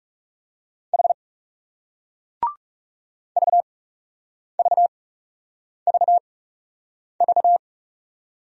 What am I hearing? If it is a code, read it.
Morse code, heard as a keyed tone, about 39 wpm, spaced as for 8 wpm, 700 Hz. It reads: H VVVV